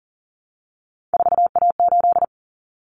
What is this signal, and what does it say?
Morse code, heard as a keyed tone, about 40 wpm, 700 Hz. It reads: 4A8
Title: Common Cut Numbers - 3-Digit 40wpm